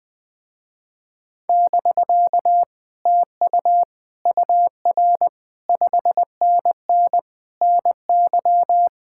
Morse code, heard as a keyed tone, about 20 wpm, 700 Hz. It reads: <BK> TU UR 5NN NY
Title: Speed-Racing Common QSO Phrases Encore - 17wpm